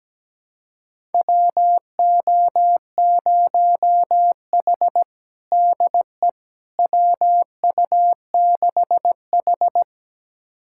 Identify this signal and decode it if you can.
Morse code, heard as a keyed tone, about 17 wpm, 700 Hz. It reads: WO0H DE WU6H
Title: USA Callsign DE Callsign - 4 Characters - 17wpm